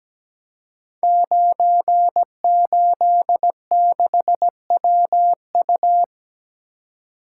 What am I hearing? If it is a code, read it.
Morse code, heard as a keyed tone, about 17 wpm, 700 Hz. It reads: 986WU
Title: Code Groups - 5 Letter:Numbers - 17wpm